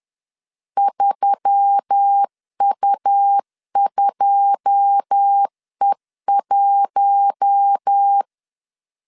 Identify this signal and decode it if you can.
Morse code, heard as a keyed tone, about 10 wpm, 800 Hz. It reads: 3U2E1
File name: CW / DET / 1 9 1500C S C 10wpm 4557